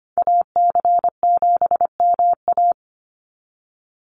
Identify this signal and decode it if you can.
Morse code, heard as a keyed tone, about 25 wpm, 690 Hz. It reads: AC7MA
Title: NA SOTA Activator Callsigns 25wpm